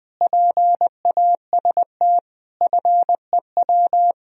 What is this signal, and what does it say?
Morse code, heard as a keyed tone, about 21 wpm, 700 Hz. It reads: PAST FEW
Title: Speed-Racing - Sets of 5 Words 20wpm